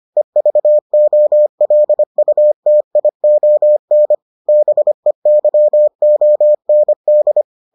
Morse code, heard as a keyed tone, about 25 wpm, 600 Hz. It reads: EVOLUTION BEYOND